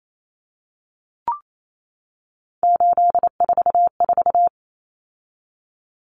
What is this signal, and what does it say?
Morse code, heard as a keyed tone, about 28 wpm, 700 Hz. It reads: 844
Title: Common Cut Numbers - 3-Digit 28wpm